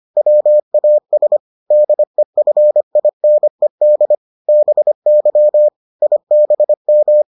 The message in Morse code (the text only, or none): WAS DEFINED BY IBM